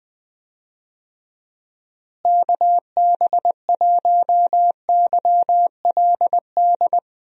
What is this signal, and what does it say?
Morse code, heard as a keyed tone, about 20 wpm, 700 Hz. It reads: KB1YLD
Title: USA Call Signs - 6 Characters - Repeated 2x - 20wpm